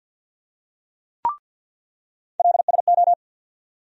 Morse code, heard as a keyed tone, about 47 wpm, 700 Hz. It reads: LSO
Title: USA Call Sign Postfixes - 3 Letters 50wpm